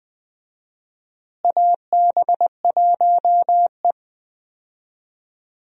Morse code, heard as a keyed tone, about 20 wpm, 700 Hz. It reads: AB1E
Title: USA Call Signs - 4 Characters - Repeated 3x - 20wpm